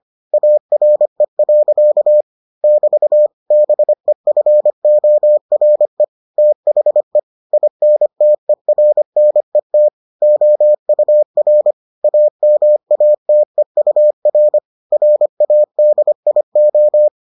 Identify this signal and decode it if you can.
Morse code, heard as a keyed tone, about 25 wpm, 600 Hz. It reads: ARE. <BT> BEFORE THE INTERNET OUR AMATEUR RADIO